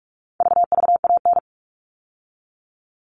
Morse code, heard as a keyed tone, about 44 wpm, 700 Hz. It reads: VVAN
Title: All Cut Numbers - 4-Digit 45wpm